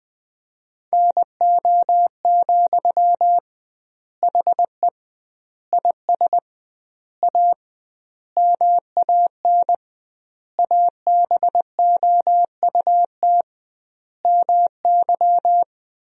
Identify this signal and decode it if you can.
Morse code, heard as a keyed tone, about 20 wpm, 700 Hz. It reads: NO, HE IS A MAN ABOUT MY
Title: 2x Word-Spacing - Sentences from Top 400 Words - 20wpm